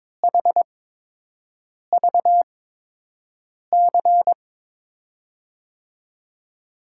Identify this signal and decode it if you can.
Morse code, heard as a keyed tone, about 22 wpm, spaced as for 6 wpm, 700 Hz. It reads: HVC
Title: USA Call Signs - 5 Characters - 8x Character Spacing - 22wpm